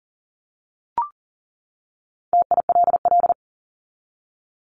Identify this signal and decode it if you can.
Morse code, heard as a keyed tone, about 40 wpm, 700 Hz. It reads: TILL
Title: Top 1751-2000 Words 40wpm